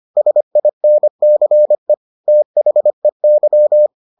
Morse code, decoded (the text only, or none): SINCE THEY